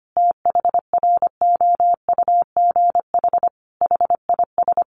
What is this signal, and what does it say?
Morse code, heard as a keyed tone, about 25 wpm, 700 Hz. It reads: THROUGH HIS
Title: Sentences from Top 1200 Words 25wpm